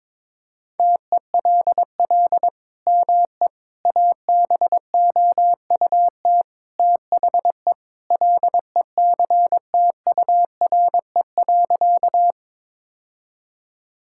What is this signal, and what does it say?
Morse code, heard as a keyed tone, about 22 wpm, 700 Hz. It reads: TELL ME ABOUT THE LECTURE.
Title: Sentences from Top 1600 Words 22wpm